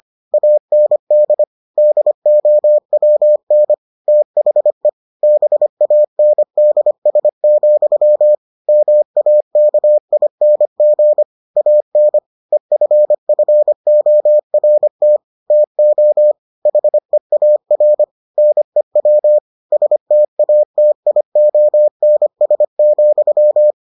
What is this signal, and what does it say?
Morse code, heard as a keyed tone, about 25 wpm, 600 Hz. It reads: AND DOWN THE BANDS, MAKING AN EFFORT TO HEAR NEW STATIONS,